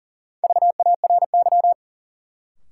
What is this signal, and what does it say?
Morse code, heard as a keyed tone, about 40 wpm, 700 Hz. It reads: VARY
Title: Sentences from Top 1200 Words - New Word Drill - 40wpm